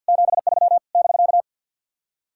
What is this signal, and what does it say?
Morse code, heard as a keyed tone, about 58 wpm, 700 Hz. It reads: 73 <BK>
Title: Speed-Racing Common QSO Phrases - 50wpm